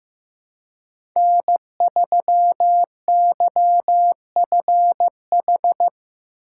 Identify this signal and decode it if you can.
Morse code, heard as a keyed tone, about 15 wpm, 700 Hz. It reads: N3YFH
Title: USA Call Signs - 5 Characters - 3x Character Spacing - 15wpm